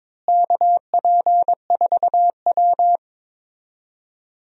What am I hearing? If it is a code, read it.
Morse code, heard as a keyed tone, about 22 wpm, 700 Hz. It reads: KP4W
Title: DXCC-100 Call Signs - 4 Characters - Repeated 3x - 22wpm